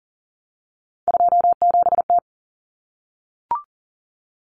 Morse code, heard as a keyed tone, about 40 wpm, 700 Hz. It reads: 27T
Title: Common Cut Numbers - 3-Digit 40wpm